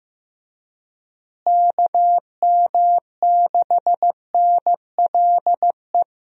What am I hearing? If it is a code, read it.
Morse code, heard as a keyed tone, about 15 wpm, 700 Hz. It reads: KM6NLE
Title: USA Call Signs - 6 Characters - Repeated 3x - 15wpm